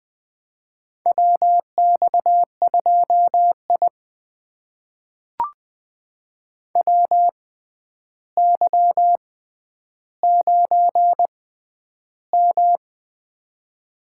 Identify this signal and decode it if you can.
Morse code, heard as a keyed tone, about 20 wpm, spaced as for 10 wpm, 700 Hz. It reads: WX2I WY9M
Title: USA Call Signs - 4 Characters - 6x Character Spacing - 20wpm